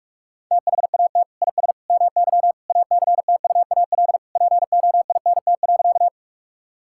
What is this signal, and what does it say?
Morse code, heard as a keyed tone, about 46 wpm, 690 Hz. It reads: THAT IS MY ACTUAL POINT.
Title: Sentences from Top 1000 Words - Review Part 5 - 45wpm